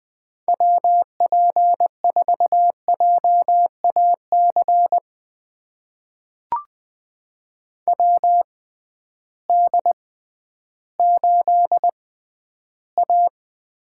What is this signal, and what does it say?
Morse code, heard as a keyed tone, about 20 wpm, spaced as for 12 wpm, 700 Hz. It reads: WP4JAC WD8A
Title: USA Call Signs - 6 Characters - 6x Character Spacing - 20wpm